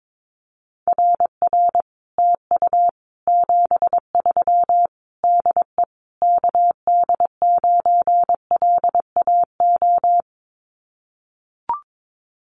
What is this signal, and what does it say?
Morse code, heard as a keyed tone, about 22 wpm, 700 Hz. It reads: RR TU 73 DE KD9LAO
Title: Common QSO Phrases Encore 22wpm